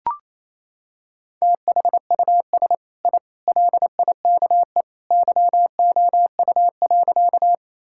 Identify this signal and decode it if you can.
Morse code, heard as a keyed tone, about 28 wpm, 700 Hz. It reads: THUS I LIKE YOU.
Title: Sentences from Top 1000 Words - Review Part 5 - 28wpm